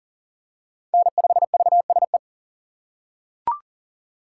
Morse code, heard as a keyed tone, about 38 wpm, 700 Hz. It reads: N5VSE
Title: USA Call Signs - 5 Characters - 3x Character Spacing - 40wpm